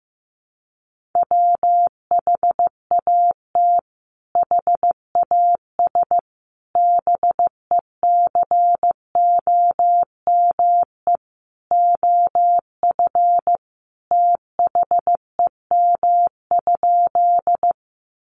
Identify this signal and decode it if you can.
Morse code, heard as a keyed tone, about 15 wpm, 700 Hz. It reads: WHAT HAS BECOME OF THEM?
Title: Sentences from Top 600 Words 15wpm